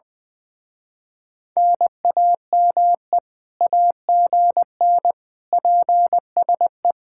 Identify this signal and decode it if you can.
Morse code, heard as a keyed tone, about 23 wpm, 700 Hz. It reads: NAME AGN PSE
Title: Speed-Racing Common QSO Phrases Encore - 20wpm